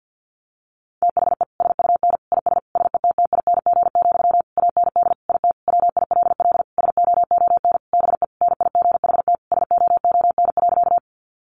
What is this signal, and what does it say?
Morse code, heard as a keyed tone, about 48 wpm, 700 Hz. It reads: THE SUN IS SETTING, AND IT WILL SOON BE NIGHT SOON.
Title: Sentences from Top 1300 Words 50wpm